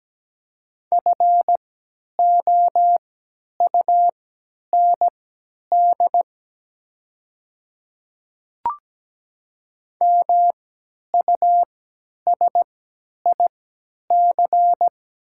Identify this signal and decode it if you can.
Morse code, heard as a keyed tone, about 17 wpm, spaced as for 10 wpm, 700 Hz. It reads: FOUND MUSIC
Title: 3x Character Spacing - Top 300 Words - 17wpm